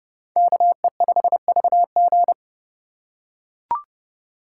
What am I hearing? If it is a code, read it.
Morse code, heard as a keyed tone, about 28 wpm, 700 Hz. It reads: KE5VG